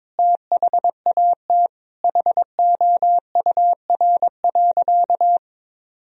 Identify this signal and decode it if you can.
Morse code, heard as a keyed tone, about 22 wpm, 700 Hz. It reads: THAT HOUR.
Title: Sentences from Top 1000 Words - Review Part 3 - 22wpm